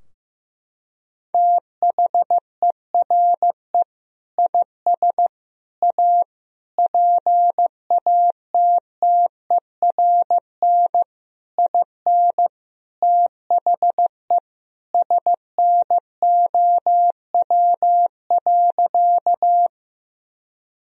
Morse code, heard as a keyed tone, about 15 wpm, 700 Hz. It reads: THERE IS A PATTERN IN THE SNOW.
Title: Sentences from Top 1000 Words - Review Part 2 - 15wpm